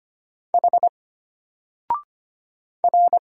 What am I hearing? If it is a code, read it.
Morse code, heard as a keyed tone, about 25 wpm, 700 Hz. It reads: H R
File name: Single-Letter-Number-25wpm